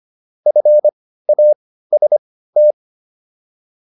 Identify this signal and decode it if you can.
Morse code, heard as a keyed tone, about 25 wpm, spaced as for 15 wpm, 600 Hz. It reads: FAST